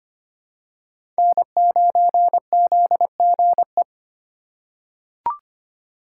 Morse code, heard as a keyed tone, about 25 wpm, 700 Hz. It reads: N9ZGE
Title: USA Call Signs - 5 Characters - 2x Character Spacing - 25wpm